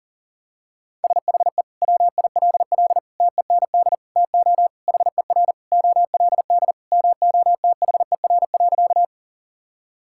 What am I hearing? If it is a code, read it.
Morse code, heard as a keyed tone, about 40 wpm, 700 Hz. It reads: SHE WILL TEND TO HER OLD MOTHER.
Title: Sentences from Top 1000 Words - Review Part 5 - 40wpm